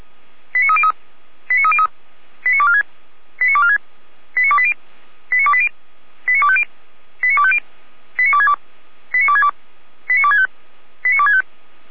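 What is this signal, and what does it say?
Five-tone selective call (ZVEI), heard as quick runs of five tones, tones of about 70 ms. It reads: 89303 89303 89246 89246 89280 89280 89250 89250 89262 89262 89265 89265
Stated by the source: ZVEI five-tone selective call 89208 received on an SDR.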